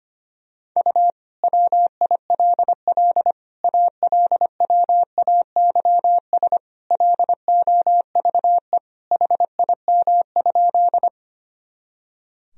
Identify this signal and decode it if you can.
Morse code, heard as a keyed tone, about 25 wpm, 700 Hz. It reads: U WILL ALWAYS LOVE HIM?
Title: Sentences from Top 400 Words and Abbreviations 25wpm